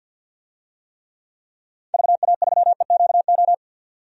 Morse code, heard as a keyed tone, about 46 wpm, 700 Hz. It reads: VA3EQO
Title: DXCC-100 Call Signs - 6 Characters - 50wpm